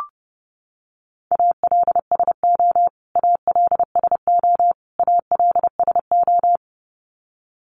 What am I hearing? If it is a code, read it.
Morse code, heard as a keyed tone, about 30 wpm, 700 Hz. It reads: ALSO ALSO ALSO
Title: Top 500 Words Repeated 30wpm